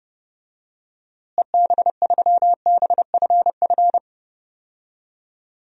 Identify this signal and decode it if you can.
Morse code, heard as a keyed tone, about 32 wpm, 700 Hz. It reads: EB3BFF
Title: DXCC-100 Call Signs - 6 Characters - Repeated 3x - 30wpm